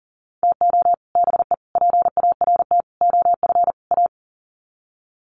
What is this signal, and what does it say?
Morse code, heard as a keyed tone, about 40 wpm, 700 Hz. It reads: TO BE PART OF A
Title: Sets of 5 Words Encore2 40wpm